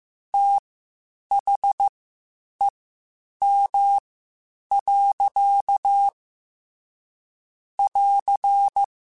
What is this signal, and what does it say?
Morse code, heard as a keyed tone, about 15 wpm, spaced as for 8 wpm, 800 Hz. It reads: THEM. <AR>
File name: Spaced / Chuckles Spaced-mp3s / SoftWords sp @13